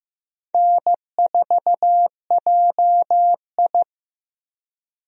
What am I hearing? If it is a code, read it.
Morse code, heard as a keyed tone, about 15 wpm, 700 Hz. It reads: N4JI
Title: USA Call Signs - 4 Characters - Repeated 2x - 15wpm